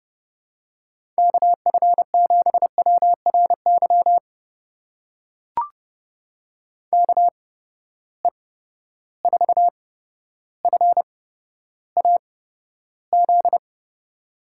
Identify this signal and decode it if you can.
Morse code, heard as a keyed tone, about 29 wpm, spaced as for 13 wpm, 700 Hz. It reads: KF7WRY KE4FAZ
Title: USA Call Signs - 6 Characters - 8x Character Spacing - 30wpm